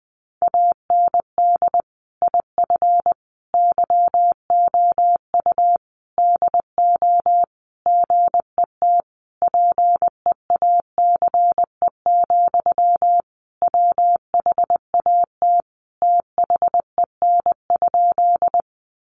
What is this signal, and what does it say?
Morse code, heard as a keyed tone, about 19 wpm, 700 Hz. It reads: AND IF YOU DO GET PEACE, WHAT THEN?
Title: Sentences from Top 1500 Words 20wpm